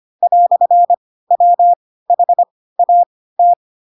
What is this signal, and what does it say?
Morse code, heard as a keyed tone, about 25 wpm, spaced as for 16 wpm, 700 Hz. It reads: "WHAT
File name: Pinocchio / Pinocchio ch3 10 25 ik1hge